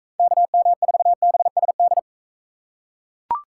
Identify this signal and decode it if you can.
Morse code, heard as a keyed tone, about 39 wpm, 700 Hz. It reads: KM4BSD